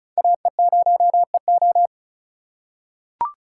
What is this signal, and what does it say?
Morse code, heard as a keyed tone, about 34 wpm, 700 Hz. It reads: AE0EO
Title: USA Call Signs - 5 Characters - 4x Character Spacing - 35wpm